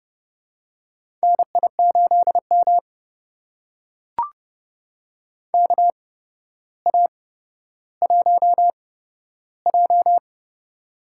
Morse code, heard as a keyed tone, about 30 wpm, spaced as for 13 wpm, 700 Hz. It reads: NI8M KA1J
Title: USA Call Signs - 8x Character Spacing - 30wpm